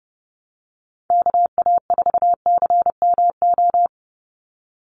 Morse code, heard as a keyed tone, about 30 wpm, 700 Hz. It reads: KA4CMO
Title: USA Call Signs 30wpm